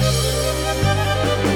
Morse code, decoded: none